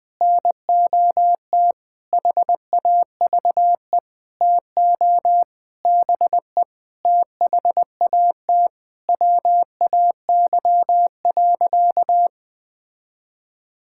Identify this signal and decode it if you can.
Morse code, heard as a keyed tone, about 20 wpm, 700 Hz. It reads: NOT HAVE TO BE THAT WAY.
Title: Sentences with 8-Word Phrases - 20wpm